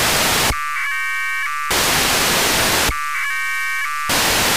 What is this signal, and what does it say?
Packet radio (AX.25, as in APRS).
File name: Horyu-4 10h28 UTC AFSK and CW JE9PEL